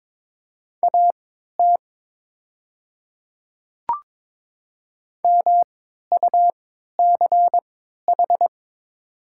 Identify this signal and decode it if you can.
Morse code, heard as a keyed tone, about 22 wpm, spaced as for 12 wpm, 700 Hz. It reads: AT MUCH
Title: 3x Character Spacing - Top 200 Words - 22wpm